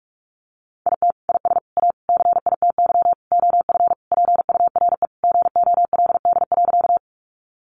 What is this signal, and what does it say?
Morse code, heard as a keyed tone, about 44 wpm, 700 Hz. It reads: IT IS A CITY OF PURE GOLD.
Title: Sentences from Top 1500 Words 45wpm